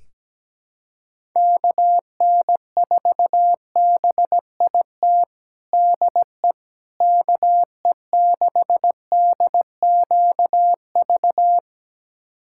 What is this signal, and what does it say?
Morse code, heard as a keyed tone, about 17 wpm, 700 Hz. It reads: KN4BIT DE KE6DQV
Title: USA Callsign DE Callsign - 6 Characters - 17wpm